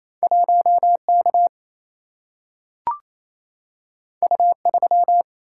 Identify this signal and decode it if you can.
Morse code, heard as a keyed tone, about 28 wpm, 700 Hz. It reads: 1K U3
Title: Single-Letter Single-Number 28wpm